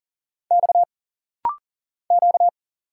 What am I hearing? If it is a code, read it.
Morse code, heard as a keyed tone, about 39 wpm, 700 Hz. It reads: X Q
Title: Single Letters - Rapid-Fire 40wpm